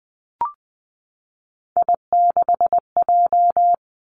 Morse code, heard as a keyed tone, about 20 wpm, 700 Hz. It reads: I6J
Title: Three Challenging Letter-Number Combinations 20wpm